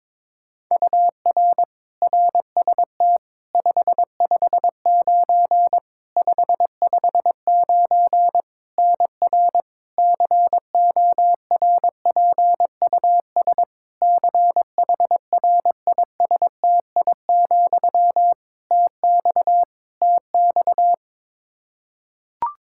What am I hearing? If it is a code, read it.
Morse code, heard as a keyed tone, about 22 wpm, 700 Hz. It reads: UR RST 559 559 NR CORPUS CHRISTI, TX TX
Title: K1USN SST QSO Practice 22wpm